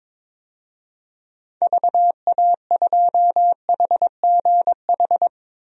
Morse code, heard as a keyed tone, about 22 wpm, 700 Hz. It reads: VA2HGH
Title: DXCC-100 Call Signs - 6 Characters - Repeated 2x - 22wpm